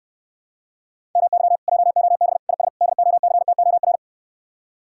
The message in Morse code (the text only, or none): MY JOB IS DOCTOR